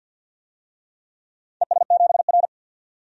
Morse code, heard as a keyed tone, about 51 wpm, 700 Hz. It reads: ES7R
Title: DXCC-100 Call Signs - 4 Characters - 50wpm